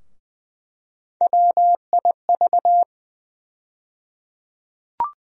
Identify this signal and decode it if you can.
Morse code, heard as a keyed tone, about 21 wpm, 700 Hz. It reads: WIV
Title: USA Call Sign Postfixes - 3 Letters 20wpm